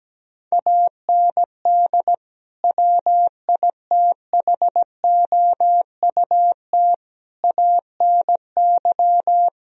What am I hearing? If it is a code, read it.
Morse code, heard as a keyed tone, about 17 wpm, 700 Hz. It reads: AND WITHOUT ANY
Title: Sentences from Top 1000 Words - Review Part 4 - 17wpm